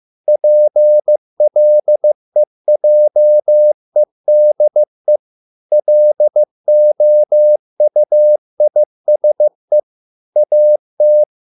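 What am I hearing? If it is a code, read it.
Morse code, heard as a keyed tone, about 15 wpm, 600 Hz. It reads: PLEJEDE LOUISE AT
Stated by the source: Morse code at a slow 15 wpm from CW med Gnister.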